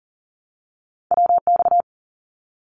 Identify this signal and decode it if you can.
Morse code, heard as a keyed tone, about 38 wpm, 700 Hz. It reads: WX
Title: Abbreviations 40wpm